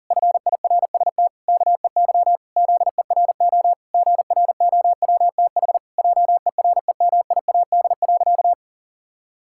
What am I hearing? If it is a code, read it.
Morse code, heard as a keyed tone, about 40 wpm, 700 Hz. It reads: FIRST KEY ZERO GROWTH JEREMIAD.
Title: Pangram Challenge v2 - 40wpm